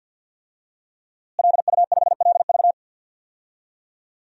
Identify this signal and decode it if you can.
Morse code, heard as a keyed tone, about 47 wpm, 700 Hz. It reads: LU5LV
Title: DXCC-100 Call Signs - 5 Characters - Repeated 3x - 50wpm